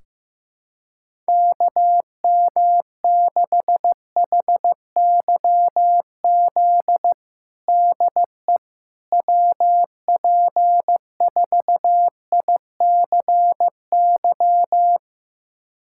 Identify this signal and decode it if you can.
Morse code, heard as a keyed tone, about 15 wpm, 700 Hz. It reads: KM6HYZ DE WP4ICY
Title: USA Callsign DE Callsign - 6 Characters - 15wpm